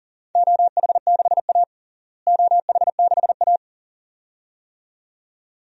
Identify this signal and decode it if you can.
Morse code, heard as a keyed tone, about 40 wpm, 700 Hz. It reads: OH6A OH6A
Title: DXCC-100 Call Signs - 4 Characters - Repeated 2x - 40wpm